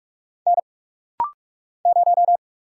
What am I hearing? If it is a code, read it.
Morse code, heard as a keyed tone, about 44 wpm, 700 Hz. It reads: N 0
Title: Single Letter-Number - Rapid-Fire 45wpm